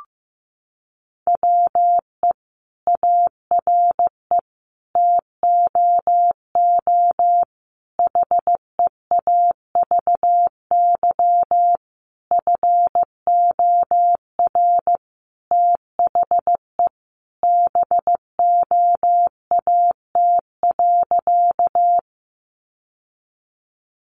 Morse code, heard as a keyed tone, about 15 wpm, 700 Hz. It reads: WE ARE TOO HEAVY FOR THE BOAT.
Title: Sentences from Top 1100 Words 15wpm